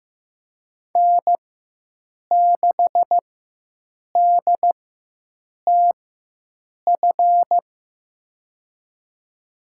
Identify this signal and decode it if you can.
Morse code, heard as a keyed tone, about 15 wpm, spaced as for 7 wpm, 700 Hz. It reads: N6DTF
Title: USA Call Signs - 5 Characters - 4x Character Spacing - 15wpm